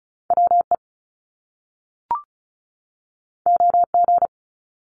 Morse code, heard as a keyed tone, about 36 wpm, 700 Hz. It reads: W E O G
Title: Top Two Letter Combinations in Words 35wpm